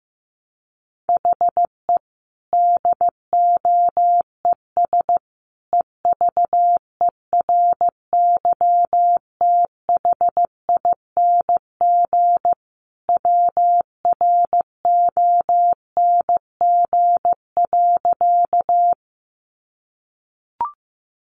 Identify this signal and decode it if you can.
Morse code, heard as a keyed tone, about 15 wpm, 700 Hz. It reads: HE DOES EVERYTHING WRONG.
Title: Sentences from Top 900 Words 15wpm